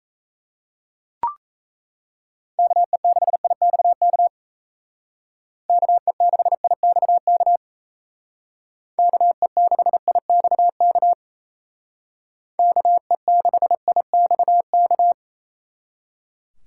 Morse code, heard as a keyed tone, about 36 wpm, 700 Hz. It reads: KE6IXK KE6IXK KE6IXK KE6IXK